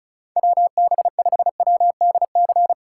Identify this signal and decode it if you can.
Morse code, heard as a keyed tone, about 34 wpm, 700 Hz. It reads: WB5WDC
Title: USA Call Signs - 6 Characters - 8x Character Spacing - 35wpm